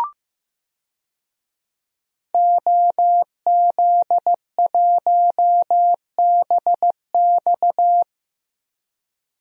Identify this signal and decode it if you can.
Morse code, heard as a keyed tone, about 15 wpm, 700 Hz. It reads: OZ1BX
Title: DXCC-100 Call Signs - 5 Characters - 15wpm